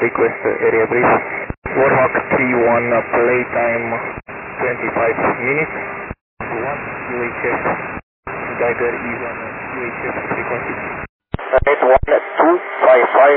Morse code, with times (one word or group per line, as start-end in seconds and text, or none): none